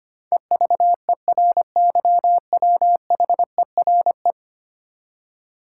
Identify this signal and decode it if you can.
Morse code, heard as a keyed tone, about 25 wpm, 700 Hz. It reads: EVERYWHERE
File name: Ten Letter Words - 25wpm